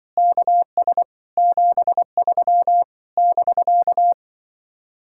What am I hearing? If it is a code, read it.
Morse code, heard as a keyed tone, about 24 wpm, 700 Hz. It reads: KS 73 <BK>
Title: Speed-Racing Common QSO Phrases - 20wpm